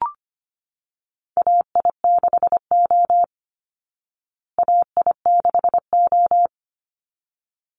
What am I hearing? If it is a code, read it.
Morse code, heard as a keyed tone, about 25 wpm, 700 Hz. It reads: AI6O AI6O
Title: Call Signs Repeated 3x - 25wpm